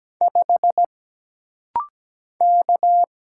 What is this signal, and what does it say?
Morse code, heard as a keyed tone, about 17 wpm, 700 Hz. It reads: 5 K
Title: Single Letter-Number - Rapid-Fire 17wpm